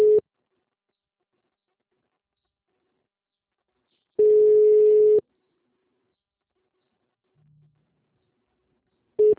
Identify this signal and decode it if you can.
Ringback tone (Europe, 425 Hz).
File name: out- -2005- - - .6354